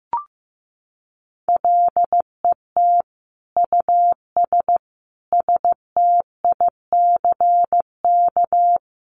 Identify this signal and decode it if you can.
Morse code, heard as a keyed tone, about 15 wpm, 700 Hz. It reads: LET US STICK
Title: Sentences from Top 1200 Words 15wpm